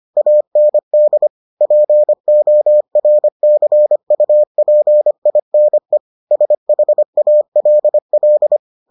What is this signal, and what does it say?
Morse code, heard as a keyed tone, about 25 wpm, 600 Hz. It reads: AND PORCUPINE SHALL